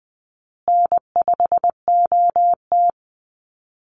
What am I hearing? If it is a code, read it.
Morse code, heard as a keyed tone, about 20 wpm, 700 Hz. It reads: N5OT
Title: Easy Call Signs Repeated 3x - 20wpm